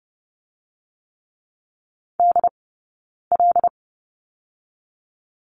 Morse code, heard as a keyed tone, about 30 wpm, 700 Hz. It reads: D L
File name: Lesson 17d - Letter u - Two Characters 30wpm